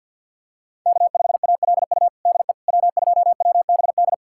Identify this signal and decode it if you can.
Morse code, heard as a keyed tone, about 52 wpm, 700 Hz. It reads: K5ALU DE W2WBD
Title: USA Callsign DE Callsign - 5 Characters - 50wpm